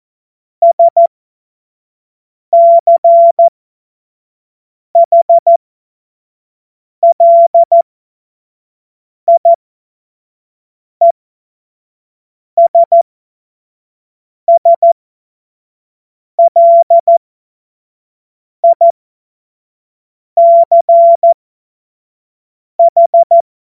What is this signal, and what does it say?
Morse code, heard as a keyed tone, about 14 wpm, spaced as for 5 wpm, 680 Hz. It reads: SCHLIESSLICH
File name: Morsekurs / Witze / Witz 11. Juli 2023 25 F70